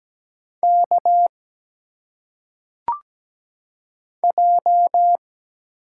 Morse code, heard as a keyed tone, about 17 wpm, 700 Hz. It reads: K J
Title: Single Letters - Phonetic Pronunciation 17wpm